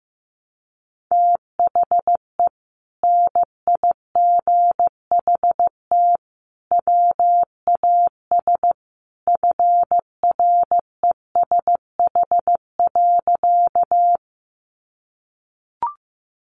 Morse code, heard as a keyed tone, about 15 wpm, 700 Hz. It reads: THE NIGHT WAS FRESH.
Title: Sentences from Top 1200 Words 15wpm